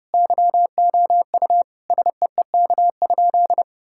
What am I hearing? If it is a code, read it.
Morse code, heard as a keyed tone, about 32 wpm, 700 Hz. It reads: YOU SEEK?
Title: Sentences from Top 1600 Words 30wpm